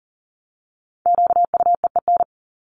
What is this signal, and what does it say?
Morse code, heard as a keyed tone, about 40 wpm, 700 Hz. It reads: QUEEN
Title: Top 1751-2000 Words 40wpm